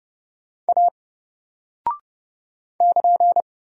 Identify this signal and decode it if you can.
Morse code, heard as a keyed tone, about 30 wpm, 700 Hz. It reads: A <KN>
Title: Single Character (Letter:Number:Punctuation:Prosign) 30wpm